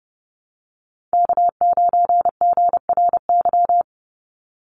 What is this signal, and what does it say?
Morse code, heard as a keyed tone, about 30 wpm, 700 Hz. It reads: K9GRY
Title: USA Call Signs 30wpm